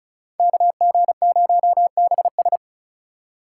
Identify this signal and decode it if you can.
Morse code, heard as a keyed tone, about 46 wpm, 700 Hz. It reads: KG0BS